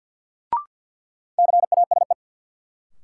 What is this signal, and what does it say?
Morse code, heard as a keyed tone, about 51 wpm, 700 Hz. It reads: CASE
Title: Sentences from Top 700 Words - New Word Drill - 50wpm